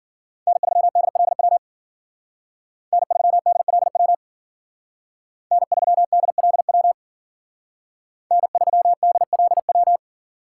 Morse code, heard as a keyed tone, about 52 wpm, 700 Hz. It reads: N3DLW N3DLW N3DLW N3DLW